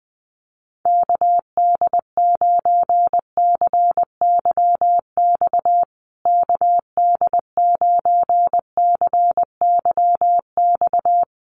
Morse code, heard as a keyed tone, about 20 wpm, 700 Hz. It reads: KD9CYX KD9CYX
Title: USA Call Signs Repeated 3x 20wpm